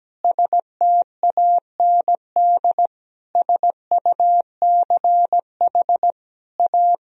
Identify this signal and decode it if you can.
Morse code, heard as a keyed tone, about 17 wpm, 700 Hz. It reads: STAND SUCH A
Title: Sentences from Top 1000 Words - Review Part 3 - 17wpm